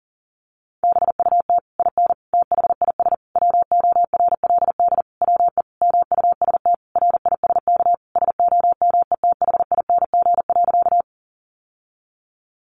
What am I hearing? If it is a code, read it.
Morse code, heard as a keyed tone, about 40 wpm, 700 Hz. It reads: BUT IN THIS WORLD WE MUST RISK SOMETHING.
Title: Sentences from Top 600 Words 40wpm